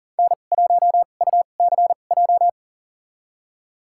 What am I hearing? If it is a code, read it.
Morse code, heard as a keyed tone, about 40 wpm, spaced as for 29 wpm, 700 Hz. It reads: N1UCJ
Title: USA Call Signs - 5 Characters - 2x Character Spacing - 40wpm